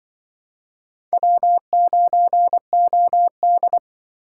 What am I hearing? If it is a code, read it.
Morse code, heard as a keyed tone, about 24 wpm, 700 Hz. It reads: W9OD
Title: Speed Racing - USA 1x2 Call Signs 20wpm